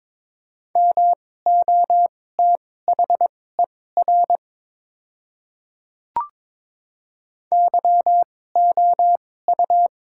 Morse code, heard as a keyed tone, about 22 wpm, spaced as for 16 wpm, 700 Hz. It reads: MOTHER YOU
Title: 2x Character Spacing - Top 200 Words - 22wpm